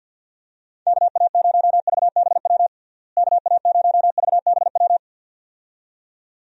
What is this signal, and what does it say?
Morse code, heard as a keyed tone, about 51 wpm, 700 Hz. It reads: KA0VBW KA0VBW
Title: USA Call Signs - 6 Characters - Repeated 2x - 50wpm